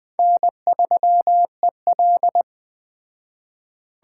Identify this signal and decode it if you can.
Morse code, heard as a keyed tone, about 20 wpm, 700 Hz. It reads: N3EL